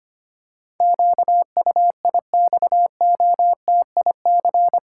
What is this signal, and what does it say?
Morse code, heard as a keyed tone, about 25 wpm, 700 Hz. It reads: QUIXOTIC
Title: Pangram Challenge v2 - 25wpm